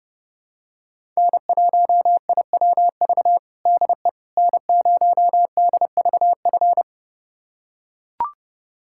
Morse code, heard as a keyed tone, about 30 wpm, 700 Hz. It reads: N1IWV DE N0DVF
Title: USA Callsign DE Callsign - 5 Characters - 30wpm